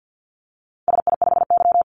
Morse code, heard as a keyed tone, about 48 wpm, 700 Hz. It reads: SI5Y
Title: Call Signs - 50wpm